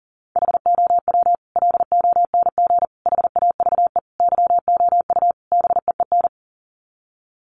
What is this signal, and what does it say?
Morse code, heard as a keyed tone, about 37 wpm, 700 Hz. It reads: HOW LONG HAVE YOU BEEN
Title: Sets of 5 Words Encore 40wpm